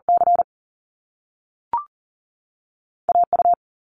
Morse code, heard as a keyed tone, about 40 wpm, 700 Hz. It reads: C AU
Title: Top Two Letter Combinations in Words 40wpm